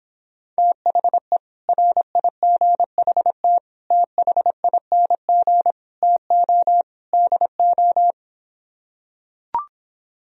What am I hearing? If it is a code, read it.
Morse code, heard as a keyed tone, about 26 wpm, 700 Hz. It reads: THE RIGHT THING TO DO